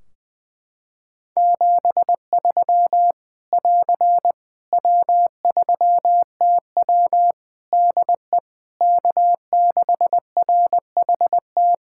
Morse code, heard as a keyed tone, about 21 wpm, 700 Hz. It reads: 73 <AR> W3TW DE K6RHT
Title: Practice QSO - Exchange 4 - Ending - 20wpm